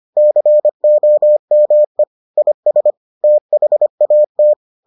Morse code, heard as a keyed tone, about 25 wpm, 600 Hz. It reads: COME IS THAT